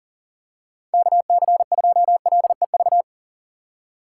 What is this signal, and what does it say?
Morse code, heard as a keyed tone, about 40 wpm, 700 Hz. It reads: KC2LEV
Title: USA Call Signs - 6 Characters - 4x Character Spacing - 40wpm